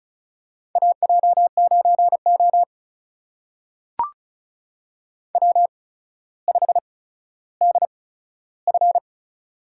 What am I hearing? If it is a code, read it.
Morse code, heard as a keyed tone, about 36 wpm, spaced as for 15 wpm, 700 Hz. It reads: AJ9O W5DF
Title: USA Call Signs - 8x Character Spacing - 35wpm